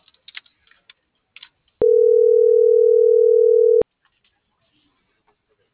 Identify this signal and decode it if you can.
Ringback tone (North America), 440 + 480 Hz.